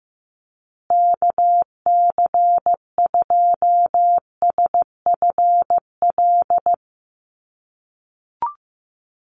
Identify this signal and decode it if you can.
Morse code, heard as a keyed tone, about 15 wpm, 700 Hz. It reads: KC2SFL
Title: USA Call Signs Repeated 2x 15wpm